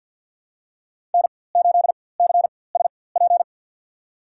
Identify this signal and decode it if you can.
Morse code, heard as a keyed tone, about 53 wpm, spaced as for 24 wpm, 700 Hz. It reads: N8CSP
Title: USA Call Signs - 5 Characters - 4x Character Spacing - 50wpm